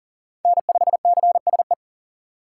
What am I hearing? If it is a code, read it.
Morse code, heard as a keyed tone, about 40 wpm, 700 Hz. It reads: N5CSE
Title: USA Call Signs - 5 Characters - 6x Character Spacing - 40wpm